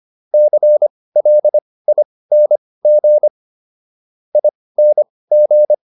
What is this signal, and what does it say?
Morse code, heard as a keyed tone, about 25 wpm, spaced as for 18 wpm, 600 Hz. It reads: CLING ING